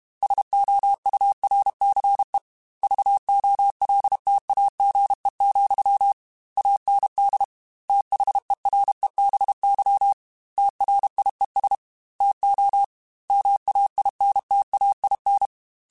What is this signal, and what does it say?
Morse code, heard as a keyed tone, about 32 wpm, 800 Hz. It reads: SOURCE VOLTAGE, AND THEREBY TRIES TO MAINTAIN
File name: Technical Subjects / Technical Subjects-mp3s / XL @30